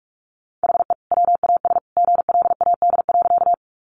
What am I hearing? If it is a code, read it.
Morse code, heard as a keyed tone, about 44 wpm, 700 Hz. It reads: HE WAS GLAD.
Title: Sentences from Top 800 Words 45wpm